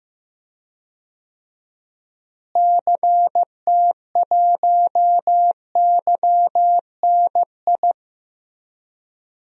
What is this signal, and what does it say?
Morse code, heard as a keyed tone, about 15 wpm, 700 Hz. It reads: CT1YNI
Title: DXCC-100 Call Signs - 6 Characters - Repeated 2x - 15wpm